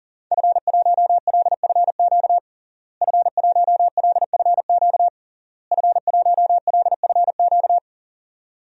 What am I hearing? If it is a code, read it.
Morse code, heard as a keyed tone, about 40 wpm, 700 Hz. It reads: F1LFQ F1LFQ F1LFQ
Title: DXCC-100 Call Signs - 5 Characters - Repeated 3x - 40wpm